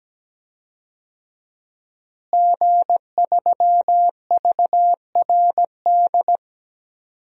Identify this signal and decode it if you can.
Morse code, heard as a keyed tone, about 17 wpm, 700 Hz. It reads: G3VRD
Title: DXCC-100 Call Signs - 5 Characters - Repeated 2x - 17wpm